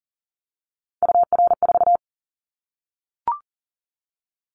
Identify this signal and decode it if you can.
Morse code, heard as a keyed tone, about 42 wpm, 700 Hz. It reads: UR4
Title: Callsign Prefixes for Top 25 DXCC - 40wpm